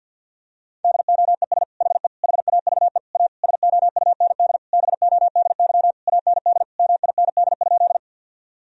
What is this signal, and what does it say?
Morse code, heard as a keyed tone, about 47 wpm, 700 Hz. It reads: DOES HE HAVE A SOUND BODY AND MIND?
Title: Sentences from Top 1000 Words - Review Part 2 - 50wpm